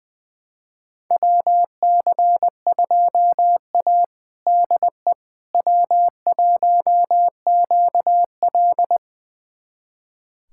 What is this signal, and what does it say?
Morse code, heard as a keyed tone, about 21 wpm, 700 Hz. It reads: WC2A DE W1QL
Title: USA Callsign DE Callsign - 4 Characters - 20wpm